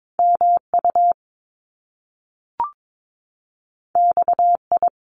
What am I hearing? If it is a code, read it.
Morse code, heard as a keyed tone, about 22 wpm, 700 Hz. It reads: MU XI
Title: Top Two Letter Combinations in Words 22wpm